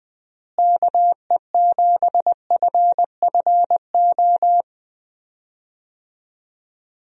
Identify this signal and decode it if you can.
Morse code, heard as a keyed tone, about 20 wpm, 700 Hz. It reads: KE7FFO
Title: USA Call Signs - 6 Characters - Repeated 3x - 20wpm